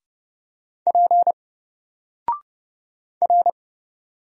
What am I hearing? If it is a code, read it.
Morse code, heard as a keyed tone, about 29 wpm, 700 Hz. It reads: P R
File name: Single-Letter-Number-30wpm